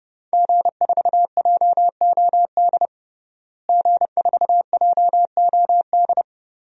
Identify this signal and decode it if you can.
Morse code, heard as a keyed tone, about 32 wpm, 700 Hz. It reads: G4JOD G4JOD
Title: DXCC-100 Call Signs - 5 Characters - Repeated 3x - 30wpm